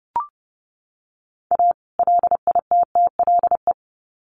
Morse code, heard as a keyed tone, about 30 wpm, 700 Hz. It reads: A LITTLE
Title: Sets of 2 Words 30wpm